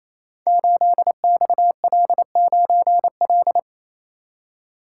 Morse code, heard as a keyed tone, about 27 wpm, 700 Hz. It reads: 8XL9L
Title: Code Groups - 5 Letter:Numbers - 28wpm